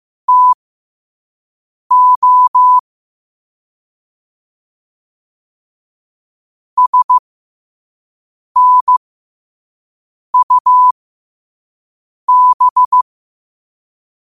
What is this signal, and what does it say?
Morse code, heard as a keyed tone, about 15 wpm, spaced as for 5 wpm, 1000 Hz. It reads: TO SNUB
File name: qotdfiles / 5wpm